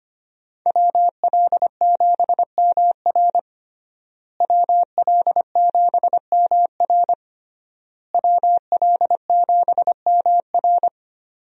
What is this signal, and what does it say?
Morse code, heard as a keyed tone, about 25 wpm, 700 Hz. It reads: WL7MR WL7MR WL7MR
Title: USA Call Signs - 5 Characters - Repeated 3x - 25wpm